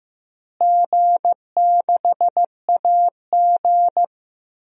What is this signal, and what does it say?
Morse code, heard as a keyed tone, about 15 wpm, 700 Hz. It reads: G6AG
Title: DXCC-100 Call Signs - 4 Characters - Repeated 3x - 15wpm